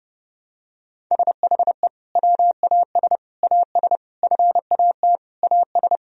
Morse code, heard as a keyed tone, about 29 wpm, 700 Hz. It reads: SHE WAS AS FAT AS
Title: Sentences from Top 1000 Words - Review Part 4 - 30wpm